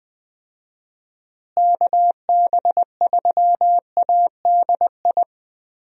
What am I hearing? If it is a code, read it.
Morse code, heard as a keyed tone, about 20 wpm, 700 Hz. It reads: KB3ADI
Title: USA Call Signs - 6 Characters - Repeated 2x - 20wpm